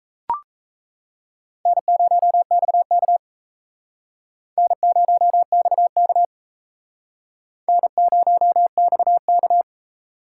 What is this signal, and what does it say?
Morse code, heard as a keyed tone, about 34 wpm, 700 Hz. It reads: N0XK N0XK N0XK
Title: Speed Racing - USA 1x2 Call Signs 28wpm